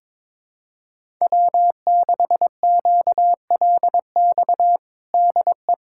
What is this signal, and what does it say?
Morse code, heard as a keyed tone, about 22 wpm, 700 Hz. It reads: W6QLX DE
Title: USA Callsign DE Callsign - 5 Characters - 22wpm